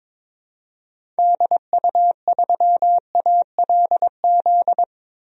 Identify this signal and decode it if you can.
Morse code, heard as a keyed tone, about 22 wpm, 700 Hz. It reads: DU3ALZ
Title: DXCC-100 Call Signs - 6 Characters - 22wpm